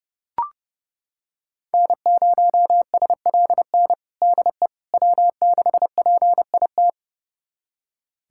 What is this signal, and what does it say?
Morse code, heard as a keyed tone, about 30 wpm, 700 Hz. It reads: N0SLN DE W6PIT
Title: USA Callsign DE Callsign - 5 Characters - 30wpm